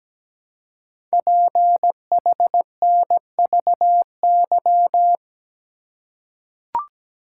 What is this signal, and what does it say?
Morse code, heard as a keyed tone, about 17 wpm, 700 Hz. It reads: PHNVY
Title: Code Groups - 5 Letters - 17wpm